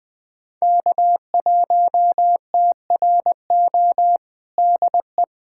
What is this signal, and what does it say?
Morse code, heard as a keyed tone, about 21 wpm, 700 Hz. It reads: K1TRO DE
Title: USA Callsign DE Callsign 20wpm